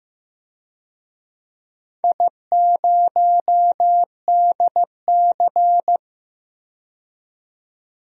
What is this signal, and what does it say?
Morse code, heard as a keyed tone, about 15 wpm, 700 Hz. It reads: I0DC
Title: DXCC-100 Call Signs - 4 Characters - 15wpm